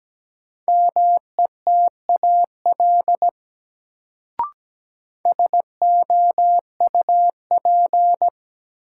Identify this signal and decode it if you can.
Morse code, heard as a keyed tone, about 17 wpm, 700 Hz. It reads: METAL SOUP
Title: Sentences from Top 1600 Words - New Word Drill Alt - 17wpm